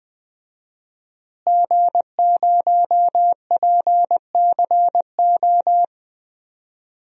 Morse code, heard as a keyed tone, about 20 wpm, 700 Hz. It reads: G0PCO
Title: DXCC-100 Call Signs - 5 Characters - Repeated 3x - 20wpm